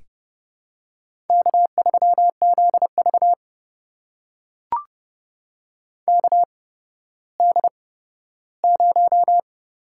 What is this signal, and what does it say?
Morse code, heard as a keyed tone, about 30 wpm, spaced as for 15 wpm, 700 Hz. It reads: K3ZV KD0
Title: USA Call Signs - 4 Characters - 8x Character Spacing - 30wpm